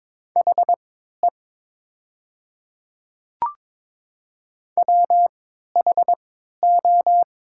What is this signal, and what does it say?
Morse code, heard as a keyed tone, about 22 wpm, spaced as for 12 wpm, 700 Hz. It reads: HE WHO
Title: 3x Character Spacing - Top 100 Words - 22wpm